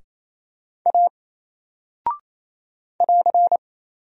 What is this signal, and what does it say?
Morse code, heard as a keyed tone, about 28 wpm, 700 Hz. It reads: A <AR>
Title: Single Character (Letter:Number:Punctuation:Prosign) Alt 28wpm